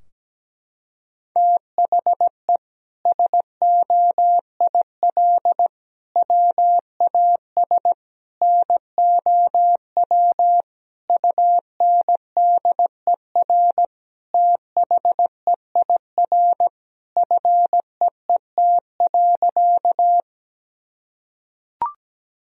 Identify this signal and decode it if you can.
Morse code, heard as a keyed tone, about 17 wpm, 700 Hz. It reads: THE SOIL WAS NOW UNDER THEIR FEET.
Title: Sentences from Top 1000 Words - Review Part 4 - 17wpm